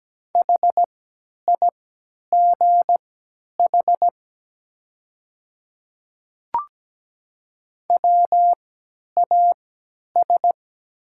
Morse code, heard as a keyed tone, about 17 wpm, spaced as for 10 wpm, 700 Hz. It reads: HIGH WAS
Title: 3x Character Spacing - Top 200 Words - 17wpm